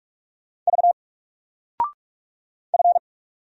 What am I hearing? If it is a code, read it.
Morse code, heard as a keyed tone, about 46 wpm, 700 Hz. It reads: V F
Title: Single Character (Letter:Number:Punctuation:Prosign) 45wpm